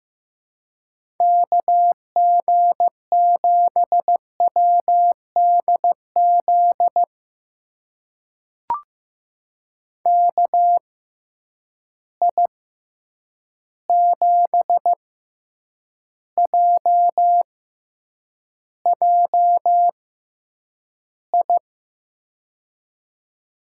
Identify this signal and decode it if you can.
Morse code, heard as a keyed tone, about 15 wpm, spaced as for 8 wpm, 700 Hz. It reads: KG7WDZ KI7JJI
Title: USA Call Signs - 6 Characters - 6x Character Spacing - 15wpm